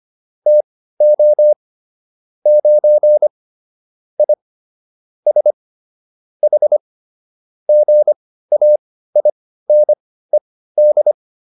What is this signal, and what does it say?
Morse code, heard as a keyed tone, about 25 wpm, spaced as for 15 wpm, 600 Hz. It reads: TO 9 I S H GAINED